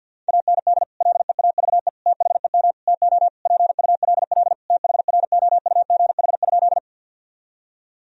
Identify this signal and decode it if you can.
Morse code, heard as a keyed tone, about 47 wpm, 700 Hz. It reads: AND LEAVE THEM TO PULL THROUGH?
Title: Sentences from Top 1000 Words - Review Part 2 - 50wpm